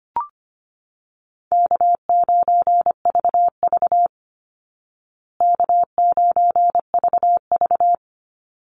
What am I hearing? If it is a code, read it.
Morse code, heard as a keyed tone, about 25 wpm, 700 Hz. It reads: K9VV K9VV
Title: Call Signs Repeated 3x - 25wpm